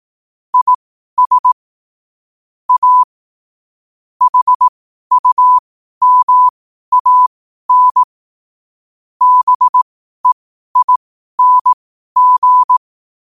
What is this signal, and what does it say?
Morse code, heard as a keyed tone, about 18 wpm, spaced as for 12 wpm, 1000 Hz. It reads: IS A HUMAN BEING
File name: qotdfiles / 12wpm